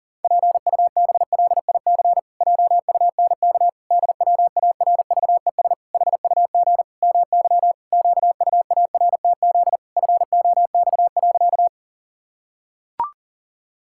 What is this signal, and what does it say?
Morse code, heard as a keyed tone, about 40 wpm, 700 Hz. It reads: PUBLIC JUNK DWARVES HUG MY QUARTZ FOX.
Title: Pangram Challenge v2 - 40wpm